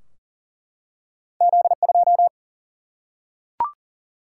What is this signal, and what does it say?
Morse code, heard as a keyed tone, about 40 wpm, 700 Hz. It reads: Z 2
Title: Single-Letter Single-Number 40wpm